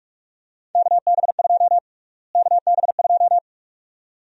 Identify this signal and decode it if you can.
Morse code, heard as a keyed tone, about 47 wpm, 700 Hz. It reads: KB2 KB2
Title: USA Call Sign Prefixes - Repeated 2x - 45wpm